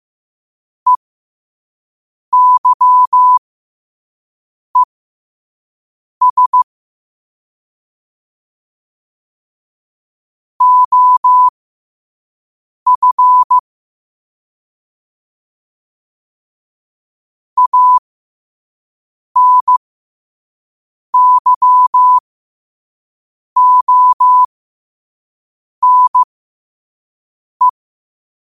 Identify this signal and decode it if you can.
Morse code, heard as a keyed tone, about 15 wpm, spaced as for 5 wpm, 1000 Hz. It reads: EYES OF ANYONE
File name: qotdfiles / 5wpm